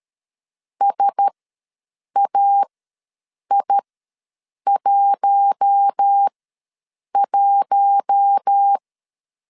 Morse code, heard as a keyed tone, about 13 wpm, spaced as for 7 wpm, 800 Hz. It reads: SAI11
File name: 5 LB ws 12wpm